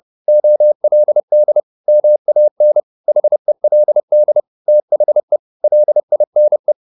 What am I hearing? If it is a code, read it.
Morse code, heard as a keyed tone, about 30 wpm, 600 Hz. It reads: OLD MAN HELD THE LINE
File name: the old man and the sea / the old man and the sea 1 30wpm